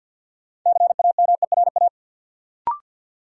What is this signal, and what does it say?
Morse code, heard as a keyed tone, about 51 wpm, 700 Hz. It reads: CAMERA
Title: Sentences from Top 1000 Words - Review Part 5 - New Word Drill - 50wpm